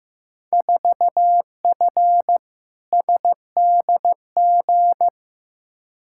Morse code, heard as a keyed tone, about 15 wpm, 700 Hz. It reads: 4F SDG
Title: Field Day Exchanges - 15wpm